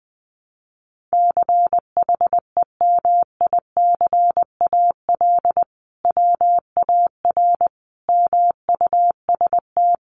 Morse code, heard as a keyed tone, about 20 wpm, 700 Hz. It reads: CHEMICAL WAR MUST
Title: Sentences from Top 1200 Words 20wpm